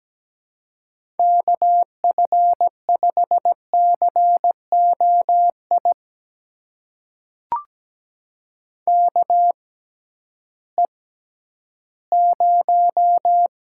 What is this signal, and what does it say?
Morse code, heard as a keyed tone, about 17 wpm, spaced as for 11 wpm, 700 Hz. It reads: KF5COI KE0
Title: USA Call Signs - 6 Characters - 6x Character Spacing - 17wpm